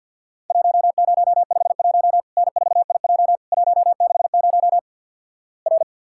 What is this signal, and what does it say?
Morse code, heard as a keyed tone, about 50 wpm, 700 Hz. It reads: 1051 N4IJ 160 R
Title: Worked All Europe DX Contest - QTC Exchange - 50wpm